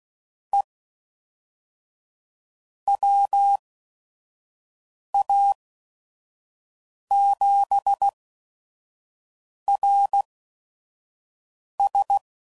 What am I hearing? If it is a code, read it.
Morse code, heard as a keyed tone, about 16 wpm, spaced as for 5 wpm, 800 Hz. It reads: E WA7RS
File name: CW Code Groups @5WPM / QSOs / QSO WA7QOX 1